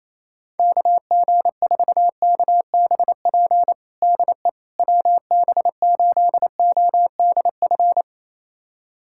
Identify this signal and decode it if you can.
Morse code, heard as a keyed tone, about 28 wpm, 700 Hz. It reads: KG4KBP DE WB8ODF
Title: USA Callsign DE Callsign 28wpm